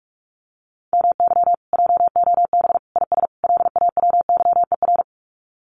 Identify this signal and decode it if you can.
Morse code, heard as a keyed tone, about 44 wpm, 700 Hz. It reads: MY JOB IS LAWYER
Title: Common QSO Phrases Encore 45wpm